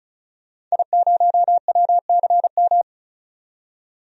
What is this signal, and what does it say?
Morse code, heard as a keyed tone, about 34 wpm, 700 Hz. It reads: I0WCM
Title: DXCC-100 Call Signs - 5 Characters - 35wpm